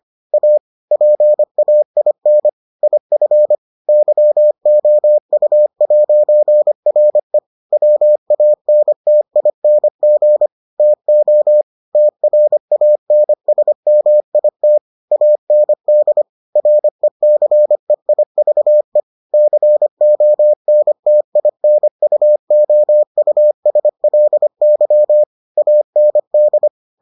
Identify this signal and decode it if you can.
Morse code, heard as a keyed tone, about 25 wpm, 600 Hz. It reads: A PAIN IF YOU'RE WANTING TO TRANSMIT AND RECEIVE CONTINUOUSLY AND